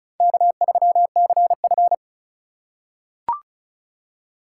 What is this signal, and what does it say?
Morse code, heard as a keyed tone, about 34 wpm, 700 Hz. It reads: K3CF
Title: USA Call Signs - 4 Characters - 4x Character Spacing - 35wpm